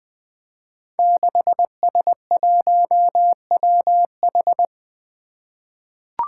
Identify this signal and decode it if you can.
Morse code, heard as a keyed tone, about 20 wpm, 700 Hz. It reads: 6S1WH
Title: Five Challenging Letter-Number Combinations 20wpm